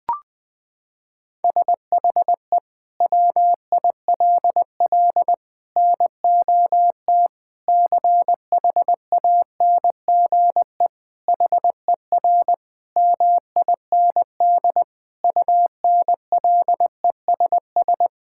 Morse code, heard as a keyed tone, about 20 wpm, 700 Hz. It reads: SHE WILL NOT CHANGE HER MIND UNLESS
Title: Sentences from Top 1000 Words - Review Part 6 - 20wpm